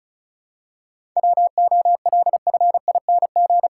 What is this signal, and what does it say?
Morse code, heard as a keyed tone, about 34 wpm, 700 Hz. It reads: WOLFING
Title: Pangram Challenge v2 - 35wpm